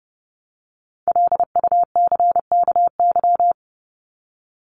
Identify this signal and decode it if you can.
Morse code, heard as a keyed tone, about 30 wpm, 700 Hz. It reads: LUCKY
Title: Top 1751-2000 Words 30wpm